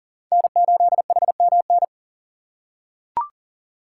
Morse code, heard as a keyed tone, about 40 wpm, 700 Hz. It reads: N8HMN
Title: USA Call Signs - 5 Characters - 4x Character Spacing - 40wpm